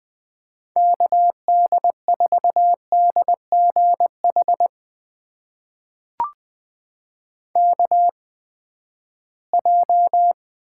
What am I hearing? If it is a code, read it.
Morse code, heard as a keyed tone, about 20 wpm, spaced as for 14 wpm, 700 Hz. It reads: KD4DGH KJ
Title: USA Call Signs - 6 Characters - 8x Character Spacing - 20wpm